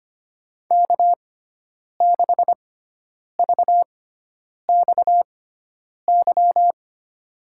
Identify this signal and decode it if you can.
Morse code, heard as a keyed tone, about 25 wpm, spaced as for 9 wpm, 700 Hz. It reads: K6VXY
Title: USA Call Signs - 6x Character Spacing - 25wpm